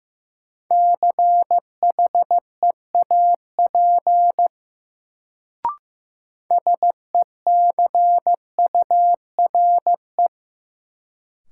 Morse code, heard as a keyed tone, about 15 wpm, 700 Hz. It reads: CHEAP SECURE
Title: Sentences from Top 1200 Words - New Word Drill Alt - 15wpm